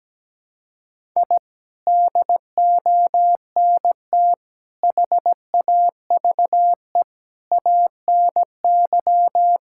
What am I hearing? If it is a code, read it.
Morse code, heard as a keyed tone, about 17 wpm, 700 Hz. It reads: I DONT HAVE ANY
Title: Sentences from Top 1000 Words - Review Part 5 - 17wpm